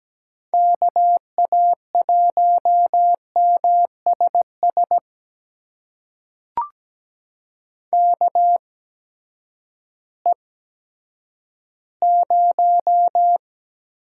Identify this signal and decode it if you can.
Morse code, heard as a keyed tone, about 17 wpm, spaced as for 10 wpm, 700 Hz. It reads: KA1MSS KE0
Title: USA Call Signs - 6 Characters - 8x Character Spacing - 17wpm